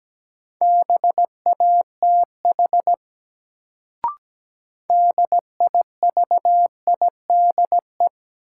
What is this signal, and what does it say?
Morse code, heard as a keyed tone, about 17 wpm, 700 Hz. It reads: BATH DIVIDE
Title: Sentences from Top 1600 Words - New Word Drill - 17wpm